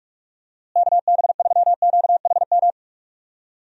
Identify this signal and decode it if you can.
Morse code, heard as a keyed tone, about 41 wpm, 700 Hz. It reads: KB3QHM
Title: USA Call Signs - 6 Characters - Repeated 3x - 45wpm